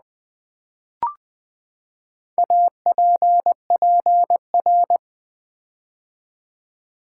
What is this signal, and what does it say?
Morse code, heard as a keyed tone, about 20 wpm, 700 Hz. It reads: APPR
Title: Build A Word - 4 Letters At A Time - 12-Letter Words - 20wpm